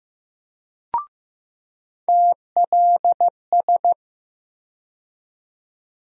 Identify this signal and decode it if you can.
Morse code, heard as a keyed tone, about 15 wpm, 700 Hz. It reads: TLS
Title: USA Call Sign Postfixes - 3 Letters 15wpm